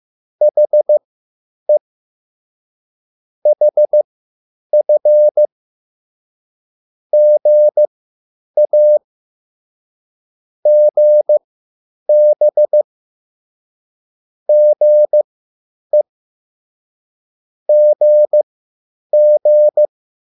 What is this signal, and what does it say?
Morse code, heard as a keyed tone, about 15 wpm, spaced as for 9 wpm, 600 Hz. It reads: HE HF GA GB GE GG